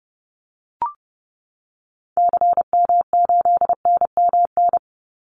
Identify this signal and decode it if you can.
Morse code, heard as a keyed tone, about 30 wpm, 700 Hz. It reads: CM8NMN
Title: Call Signs Repeated 3x - 30wpm